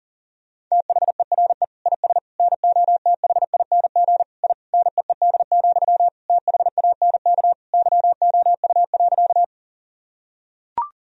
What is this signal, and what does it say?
Morse code, heard as a keyed tone, about 40 wpm, 700 Hz. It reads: THERE IS NOTHING I NEED, THANK YOU.
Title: Sentences from Top 1000 Words - Review Part 1 - 40wpm